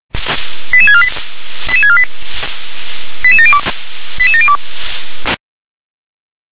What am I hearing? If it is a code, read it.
Five-tone selective call (ZVEI), heard as quick runs of five tones, tones of about 70 ms. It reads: 88648 88648 88702 88702
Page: SDR/ZVEI